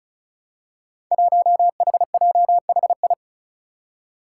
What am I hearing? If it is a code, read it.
Morse code, heard as a keyed tone, about 36 wpm, 700 Hz. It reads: 1HJHI
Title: Five Challenging Letter-Number Combinations 35wpm